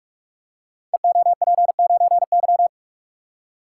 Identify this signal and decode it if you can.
Morse code, heard as a keyed tone, about 46 wpm, 700 Hz. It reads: EOP9Y
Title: Code Groups - 5 Letter:Numbers - 45wpm